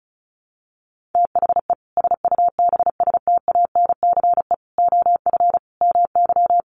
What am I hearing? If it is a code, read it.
Morse code, heard as a keyed tone, about 34 wpm, 700 Hz. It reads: THE SUBSTANCE OF MY
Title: Sentences from Top 1500 Words 35wpm